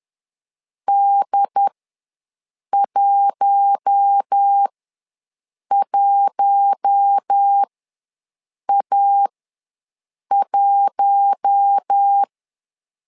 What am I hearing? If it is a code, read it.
Morse code, heard as a keyed tone, about 10 wpm, spaced as for 6 wpm, 800 Hz. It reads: D11A1
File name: LEZIONI / 4 Q1 ws 10wpm 2538